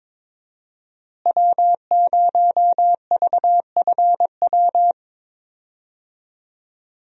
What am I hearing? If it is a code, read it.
Morse code, heard as a keyed tone, about 22 wpm, 700 Hz. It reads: W0VFW
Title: USA Call Signs - 5 Characters - Repeated 2x - 22wpm